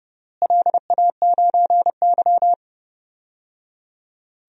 Morse code, heard as a keyed tone, about 31 wpm, 700 Hz. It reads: LA9Y
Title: DXCC-100 Call Signs - 4 Characters - Repeated 3x - 30wpm